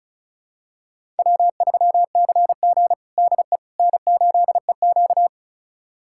Morse code, heard as a keyed tone, about 36 wpm, 700 Hz. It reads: W3CG DE N8EQ
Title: USA Callsign DE Callsign - 4 Characters - 35wpm